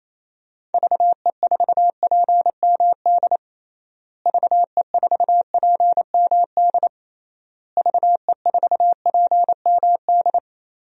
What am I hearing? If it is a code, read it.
Morse code, heard as a keyed tone, about 28 wpm, 700 Hz. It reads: VE4PMD VE4PMD VE4PMD
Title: DXCC-100 Call Signs - 6 Characters - Repeated 3x - 28wpm